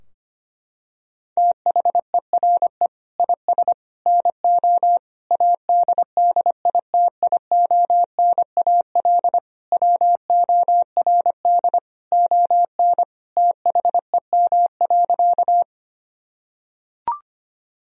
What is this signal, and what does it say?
Morse code, heard as a keyed tone, about 25 wpm, 690 Hz. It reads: THERE IS NO ADDITIONAL WORD ON THEM.
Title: Sentences from Top 1000 Words - Review Part 4 - 25wpm